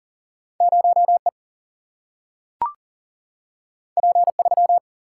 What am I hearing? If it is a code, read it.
Morse code, heard as a keyed tone, about 40 wpm, 700 Hz. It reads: 0 E P 3
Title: Single-Letter Single-Number 40wpm